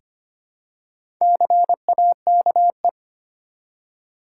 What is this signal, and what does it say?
Morse code, heard as a keyed tone, about 25 wpm, 700 Hz. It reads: CAKE
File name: practice / 25wpm / Easy Words with Hints - 25wpm